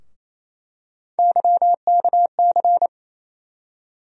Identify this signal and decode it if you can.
Morse code, heard as a keyed tone, about 27 wpm, 700 Hz. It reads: YKC
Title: USA Call Sign Postfixes - 3 Letters 28wpm